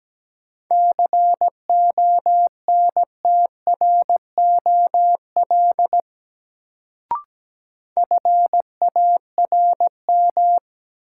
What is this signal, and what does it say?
Morse code, heard as a keyed tone, about 17 wpm, 700 Hz. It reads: CONTROL FARM
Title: Sentences from Top 1000 Words - Review Part 2 - New Word Drill - 17wpm